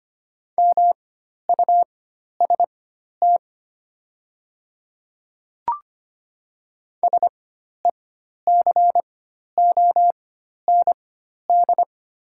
Morse code, heard as a keyed tone, about 25 wpm, spaced as for 12 wpm, 700 Hz. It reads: MUST SECOND
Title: 4x Character Spacing - Top 300 Words - 25wpm